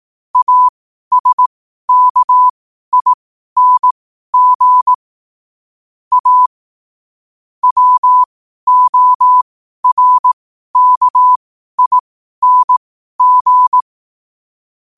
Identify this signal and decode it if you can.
Morse code, heard as a keyed tone, about 18 wpm, spaced as for 12 wpm, 1000 Hz. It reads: ASKING A WORKING
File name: qotdfiles / 12wpm